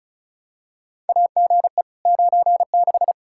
Morse code, heard as a keyed tone, about 34 wpm, 700 Hz. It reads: AGE 96
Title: Practice QSO - Exchange 3 - More Info - 35wpm